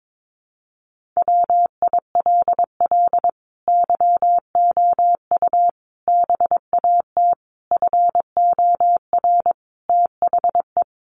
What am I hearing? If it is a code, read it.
Morse code, heard as a keyed tone, about 22 wpm, 700 Hz. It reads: WILL YOU BAT FOR THE
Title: Sentences from Top 1500 Words 22wpm